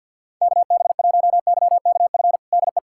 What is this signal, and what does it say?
Morse code, heard as a keyed tone, about 46 wpm, 700 Hz. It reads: KB1YKF DE
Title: USA Callsign DE Callsign - 6 Characters - 50wpm